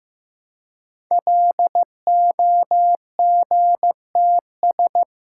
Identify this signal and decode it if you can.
Morse code, heard as a keyed tone, about 15 wpm, 700 Hz. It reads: LOGTS
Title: Code Groups - 5 Letter:Numbers - 15wpm